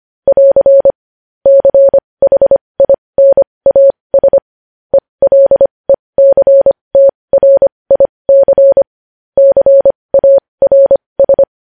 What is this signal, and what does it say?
Morse code, heard as a keyed tone, about 25 wpm, spaced as for 20 wpm, 550 Hz. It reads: <AR> CHINAS ELECTRIC CARS
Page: News Headlines in Morse Code at 20 WPM